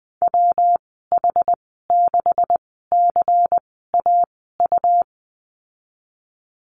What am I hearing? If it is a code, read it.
Morse code, heard as a keyed tone, about 20 wpm, spaced as for 15 wpm, 700 Hz. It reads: WH6CAU
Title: 140 Lesson 23g - Letter f - Call Signs 20wpm